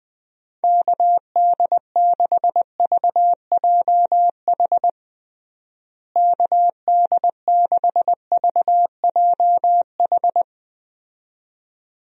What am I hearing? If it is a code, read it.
Morse code, heard as a keyed tone, about 21 wpm, 700 Hz. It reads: KD6VJH KD6VJH
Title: USA Call Signs - 6 Characters - Repeated 2x - 20wpm